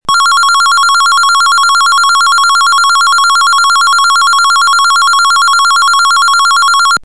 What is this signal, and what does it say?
There is no code.